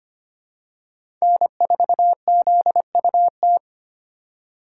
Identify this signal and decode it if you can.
Morse code, heard as a keyed tone, about 25 wpm, 700 Hz. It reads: N4ZUT
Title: USA Call Signs - 5 Characters - 3x Character Spacing - 25wpm